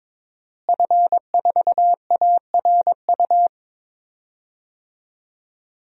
Morse code, heard as a keyed tone, about 22 wpm, 700 Hz. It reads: F4ARU
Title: DXCC-100 Call Signs - 5 Characters - Repeated 3x - 22wpm